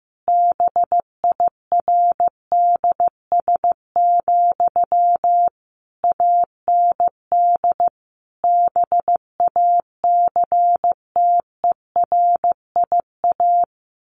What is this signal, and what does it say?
Morse code, heard as a keyed tone, about 15 wpm, 700 Hz. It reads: BIRDS, AND BACTERIA
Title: A Sound Of Thunder 15wpm